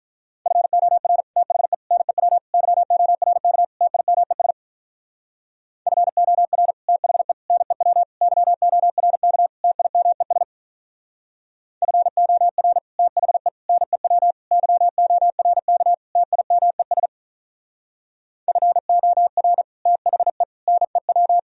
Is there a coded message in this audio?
Yes: Morse code from 0.5 to 21.4 s, heard as a keyed tone, about 44 wpm, 700 Hz. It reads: FOR THE NEW YORK TIMES FOR THE NEW YORK TIMES FOR THE NEW YORK TIMES FOR THE NEW